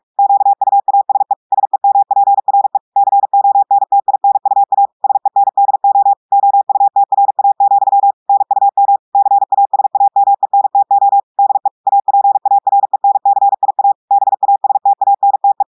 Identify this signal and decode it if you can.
Morse code, heard as a keyed tone, about 46 wpm, 800 Hz. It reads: QUASE SEMPRE CONTINUA SENDO OUTRA, NUM CASAMENTO DE APARENCIA BASTANTE
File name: portugais texte en clair 1 vitesse 45 mots minutes